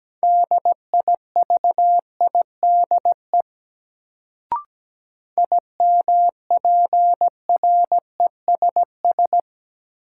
Morse code, heard as a keyed tone, about 17 wpm, 700 Hz. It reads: DIVIDE IMPRESS
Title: Sentences from Top 1600 Words - New Word Drill Alt - 17wpm